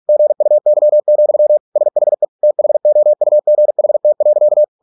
Morse code, heard as a keyed tone, about 46 wpm, 600 Hz. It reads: GUY, SHE THOUGHT.